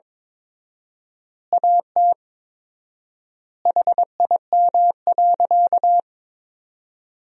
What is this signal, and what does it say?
Morse code, heard as a keyed tone, about 22 wpm, 700 Hz. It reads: AT HIM.
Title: 4x Word-Spacing - Sentences from Top 100 Words - 22wpm